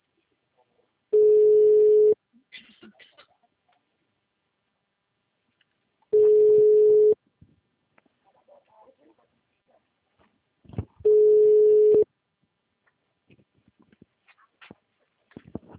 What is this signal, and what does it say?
Ringback tone (Europe, 425 Hz), with speech.
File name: out- -2005- - - .56900